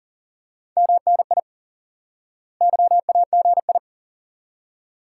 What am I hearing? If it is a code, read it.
Morse code, heard as a keyed tone, about 40 wpm, 700 Hz. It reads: MNI YAGI
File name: IQE Course 3 - Lesson 89 - YAGI - Two Elements - 40wpm